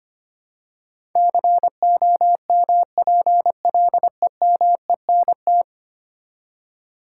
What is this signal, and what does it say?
Morse code, heard as a keyed tone, about 25 wpm, 700 Hz. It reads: COMPLEMENT
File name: Ten Letter Words - 25wpm